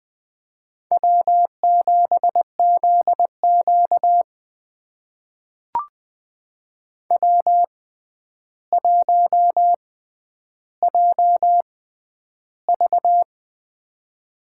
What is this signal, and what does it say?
Morse code, heard as a keyed tone, about 20 wpm, spaced as for 10 wpm, 700 Hz. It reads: W7ZQ W1JV
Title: USA Call Signs - 4 Characters - 6x Character Spacing - 20wpm